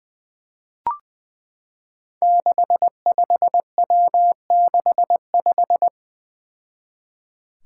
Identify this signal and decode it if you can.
Morse code, heard as a keyed tone, about 20 wpm, 700 Hz. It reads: 65W65
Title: Five Challenging Letter-Number Combinations 20wpm